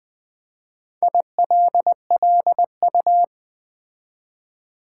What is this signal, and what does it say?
Morse code, heard as a keyed tone, about 20 wpm, 700 Hz. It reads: ILLU
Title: Build A Word - 4 Letters At A Time - 12-Letter Words - 20wpm